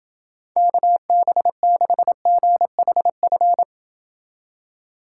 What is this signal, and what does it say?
Morse code, heard as a keyed tone, about 29 wpm, 700 Hz. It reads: KB6GHF